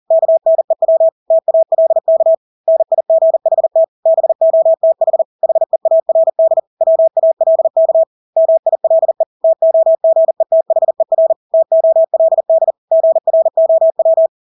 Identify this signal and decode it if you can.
Morse code, heard as a keyed tone, about 40 wpm, 650 Hz. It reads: KNEW TALK NIGHT BOTH HEARD WALK MILE TOGETHER TOLD GROW
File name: Engelske ord 40-40 wpm